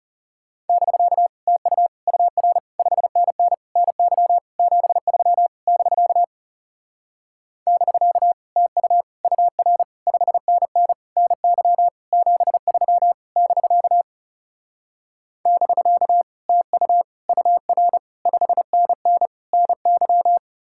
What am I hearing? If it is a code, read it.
Morse code, heard as a keyed tone, about 37 wpm, 700 Hz. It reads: <BK> TU UR 5NN NY 73 <BK> <BK> TU UR 5NN NY 73 <BK> <BK> TU UR 5NN NY
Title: Speed-Racing Common QSO Phrases - 30wpm